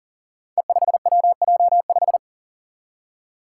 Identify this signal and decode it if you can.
Morse code, heard as a keyed tone, about 40 wpm, 700 Hz. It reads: E5WJ5
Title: Five Challenging Letter-Number Combinations 40wpm